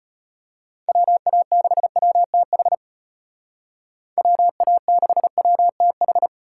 Morse code, heard as a keyed tone, about 31 wpm, 700 Hz. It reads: WA6WTH WA6WTH
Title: Speed Racing - USA Call Signs 25wpm